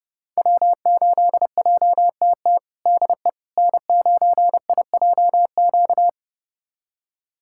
Morse code, heard as a keyed tone, about 30 wpm, 700 Hz. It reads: W8JTT DE N9IJQ
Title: USA Callsign DE Callsign - 5 Characters - 30wpm